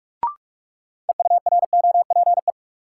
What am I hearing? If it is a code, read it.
Morse code, heard as a keyed tone, about 46 wpm, 700 Hz. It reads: EUROPE
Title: Sets of 5 Words Encore - New Word Drill - 45wpm